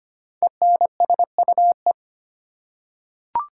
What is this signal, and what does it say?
Morse code, heard as a keyed tone, about 25 wpm, 700 Hz. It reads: ENSUE
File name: 5 Letter Words - First 3 Letters Given - 25wpm